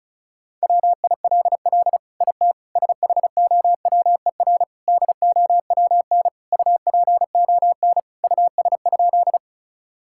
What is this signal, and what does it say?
Morse code, heard as a keyed tone, about 36 wpm, 700 Hz. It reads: WILL IT SHOWER DOWN UPON US?
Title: Sentences from Top 1600 Words 35wpm